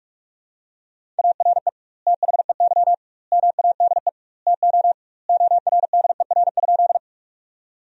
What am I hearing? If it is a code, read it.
Morse code, heard as a keyed tone, about 46 wpm, 700 Hz. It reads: ARE THEY MADE TO ORDER?
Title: 2x Word-Spacing - Sentences from Top 400 Words - 45wpm